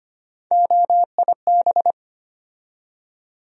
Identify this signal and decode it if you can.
Morse code, heard as a keyed tone, about 25 wpm, 700 Hz. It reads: OIB
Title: USA Call Sign Postfixes - 3 Letters 25wpm